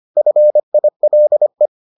Morse code, heard as a keyed tone, about 25 wpm, 600 Hz. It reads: FILE